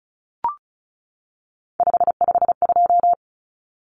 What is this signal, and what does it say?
Morse code, heard as a keyed tone, about 34 wpm, 700 Hz. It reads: 552
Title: Numbers - 3-Digit 35wpm